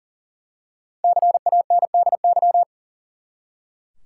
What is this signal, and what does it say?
Morse code, heard as a keyed tone, about 40 wpm, 700 Hz. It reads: CANDY
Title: Five Letter Words - 40wpm